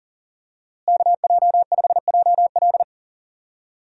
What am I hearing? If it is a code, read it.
Morse code, heard as a keyed tone, about 39 wpm, 700 Hz. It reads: KJ5JL
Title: USA Call Signs - 5 Characters - Repeated 3x - 40wpm